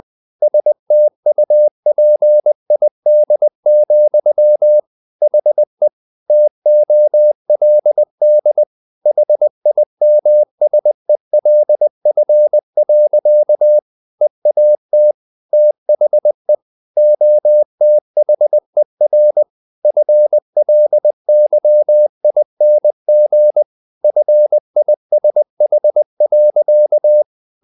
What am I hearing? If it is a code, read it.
Morse code, heard as a keyed tone, about 20 wpm, 600 Hz. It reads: STUPID, HE TOLD HIMSELF. EAT THE OTHER FLYING FISH.
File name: the old man and the sea part 2 20wpm